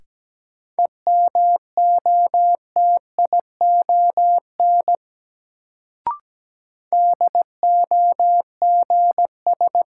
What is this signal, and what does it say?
Morse code, heard as a keyed tone, about 17 wpm, 700 Hz. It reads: EMOTION DOGS
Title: Sentences from Top 1600 Words - New Word Drill - 17wpm